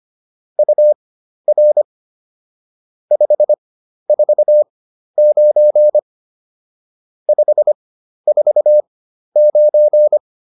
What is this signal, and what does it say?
Morse code, heard as a keyed tone, about 25 wpm, spaced as for 12 wpm, 600 Hz. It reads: UR 549 549
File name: QSO 204 12